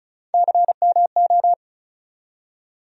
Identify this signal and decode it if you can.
Morse code, heard as a keyed tone, about 34 wpm, 700 Hz. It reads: CMO
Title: USA Call Sign Postfixes - 3 Letters 35wpm